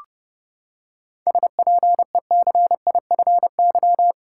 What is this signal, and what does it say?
Morse code, heard as a keyed tone, about 32 wpm, 700 Hz. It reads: SPECIFY
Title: Seven Letter Words - 30wpm